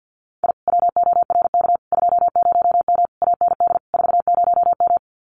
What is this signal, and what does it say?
Morse code, heard as a keyed tone, about 48 wpm, 700 Hz. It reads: I WORK 20M AND 40M
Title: Common QSO Phrases Encore 50wpm